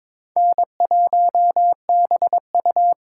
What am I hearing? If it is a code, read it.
Morse code, heard as a keyed tone, about 22 wpm, 700 Hz. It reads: N1BU
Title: USA Call Signs - 5 Characters - 8x Character Spacing - 22wpm